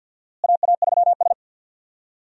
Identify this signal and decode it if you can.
Morse code, heard as a keyed tone, about 52 wpm, 700 Hz. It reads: AA3S
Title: USA Call Signs - 4 Characters - Repeated 2x - 50wpm